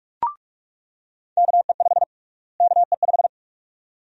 Morse code, heard as a keyed tone, about 46 wpm, 700 Hz. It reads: KE5 KE5
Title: USA Call Sign Prefixes - Repeated 2x - 45wpm